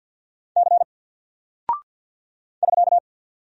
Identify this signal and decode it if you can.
Morse code, heard as a keyed tone, about 51 wpm, 700 Hz. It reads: C <SK>
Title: Single Character (Letter:Number:Punctuation:Prosign) Alt 50wpm